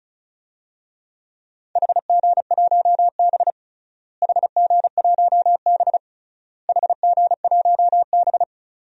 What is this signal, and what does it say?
Morse code, heard as a keyed tone, about 34 wpm, 700 Hz. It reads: HG1B HG1B HG1B
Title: DXCC-100 Call Signs - 4 Characters - Repeated 3x - 35wpm